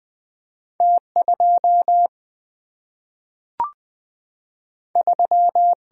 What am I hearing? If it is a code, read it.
Morse code, heard as a keyed tone, about 20 wpm, 700 Hz. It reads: T 2 3
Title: Single-Letter Single-Number 20wpm